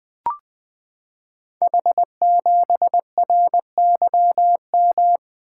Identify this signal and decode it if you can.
Morse code, heard as a keyed tone, about 20 wpm, 700 Hz. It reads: H7RYM
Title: Code Groups - 5 Letter:Numbers - 20wpm